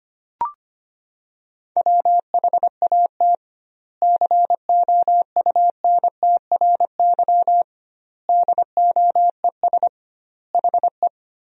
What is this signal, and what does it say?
Morse code, heard as a keyed tone, about 25 wpm, 700 Hz. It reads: WHAT COUNTRY DOES HE
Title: 2x Word-Spacing - Sentences from Top 300 Words - 25wpm